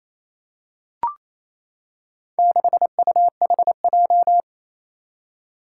Morse code, heard as a keyed tone, about 28 wpm, 700 Hz. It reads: 6UHJ
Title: Four Challenging Letter-Number Combinations 28wpm